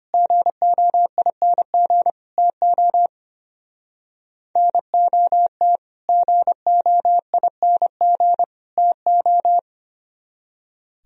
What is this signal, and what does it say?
Morse code, heard as a keyed tone, about 32 wpm, 700 Hz. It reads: GOING TO NOT GOING TO